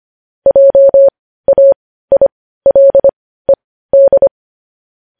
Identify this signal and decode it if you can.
Morse code, heard as a keyed tone, about 25 wpm, spaced as for 15 wpm, 550 Hz. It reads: JAILED
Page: news headlines in morse code at 15 wpm